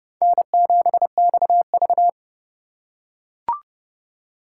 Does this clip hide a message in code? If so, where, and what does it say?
Yes: Morse code from 0.2 to 2.1 s, heard as a keyed tone, about 28 wpm, 700 Hz. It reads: N7XV